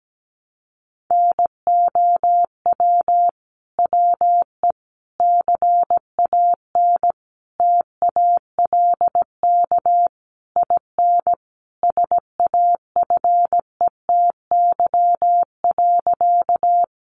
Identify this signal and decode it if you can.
Morse code, heard as a keyed tone, about 17 wpm, 700 Hz. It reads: NOW WE CAN TALK IN SAFETY.
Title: Sentences from Top 700 Words 17wpm